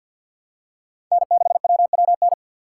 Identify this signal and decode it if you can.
Morse code, heard as a keyed tone, about 51 wpm, 700 Hz. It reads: N6WWN
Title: USA Call Signs - 3x Character Spacing - 50wpm